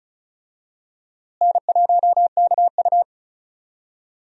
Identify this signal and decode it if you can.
Morse code, heard as a keyed tone, about 34 wpm, 700 Hz. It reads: N1KU
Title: USA Call Signs - 4 Characters - Repeated 2x - 35wpm